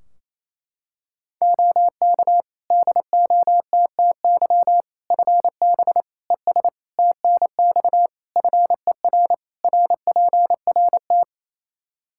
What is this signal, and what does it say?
Morse code, heard as a keyed tone, about 28 wpm, 700 Hz. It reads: OK DOTTY FB ES TNX FER RPRT
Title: K1USN SST QSO Practice 28wpm